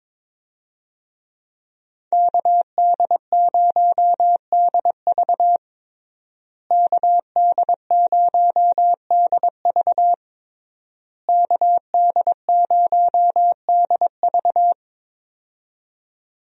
Morse code, heard as a keyed tone, about 22 wpm, 700 Hz. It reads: KD0DV KD0DV KD0DV
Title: USA Call Signs - 5 Characters - Repeated 3x - 22wpm